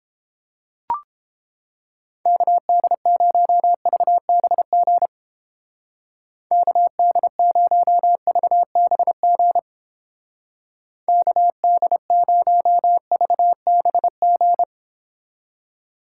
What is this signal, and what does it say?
Morse code, heard as a keyed tone, about 30 wpm, 700 Hz. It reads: KD0VBG KD0VBG KD0VBG